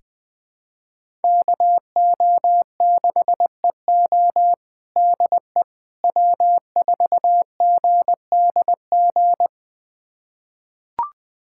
Morse code, heard as a keyed tone, about 20 wpm, 700 Hz. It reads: KO6EO DE W4GDG
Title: USA Callsign DE Callsign - 5 Characters - 20wpm